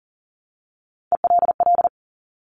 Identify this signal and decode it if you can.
Morse code, heard as a keyed tone, about 42 wpm, 690 Hz. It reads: ELL
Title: Top Three Letter Combinations in Words 40wpm